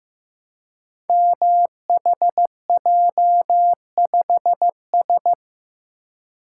Morse code, heard as a keyed tone, about 15 wpm, 700 Hz. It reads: MHJ5S
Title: Code Groups - 5 Letter:Numbers - 15wpm